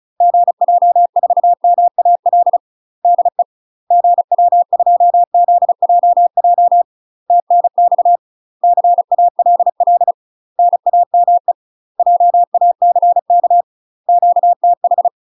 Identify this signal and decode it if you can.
Morse code, heard as a keyed tone, about 34 wpm, 700 Hz. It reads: GJ4MAL DE GW2ZJJ TNX CALL NAME JACK QTH
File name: UK MASTER QSOs TRAINING / 35WPM / UK MASTER QSOs 0000